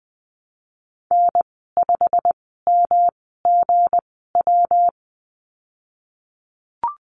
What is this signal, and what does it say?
Morse code, heard as a keyed tone, about 20 wpm, spaced as for 14 wpm, 700 Hz. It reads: N5MGW
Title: 173 Lesson 28g - Number 9 - Call Signs 20wpm